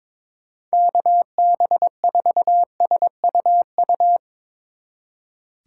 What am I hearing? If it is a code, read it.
Morse code, heard as a keyed tone, about 23 wpm, 700 Hz. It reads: KB4SUU
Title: Speed Racing - USA 2x3 Call Signs 22wpm